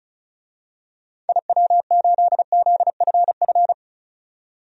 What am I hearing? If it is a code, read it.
Morse code, heard as a keyed tone, about 34 wpm, 700 Hz. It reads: IW8ZFF
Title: DXCC-100 Call Signs - 6 Characters - 35wpm